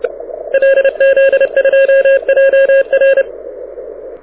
Morse code, heard as a keyed tone, about 30 wpm, 1650 Hz. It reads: LZ2JR
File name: audio zawody / IARU 2015 CW SN7Q